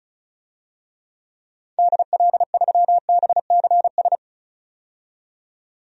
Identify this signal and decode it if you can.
Morse code, heard as a keyed tone, about 36 wpm, 700 Hz. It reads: DL3BCS
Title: DXCC-100 Call Signs - 6 Characters - 35wpm